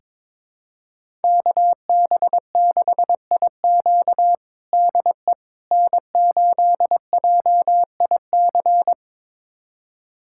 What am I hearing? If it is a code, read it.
Morse code, heard as a keyed tone, about 22 wpm, 700 Hz. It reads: KB6IQ DE N8JIC
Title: USA Callsign DE Callsign - 5 Characters - 22wpm